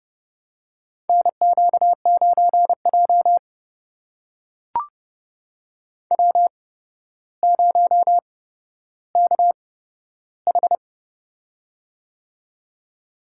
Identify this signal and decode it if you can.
Morse code, heard as a keyed tone, about 30 wpm, spaced as for 13 wpm, 700 Hz. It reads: NQ9J W0KH
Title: USA Call Signs - 4 Characters - 8x Character Spacing - 30wpm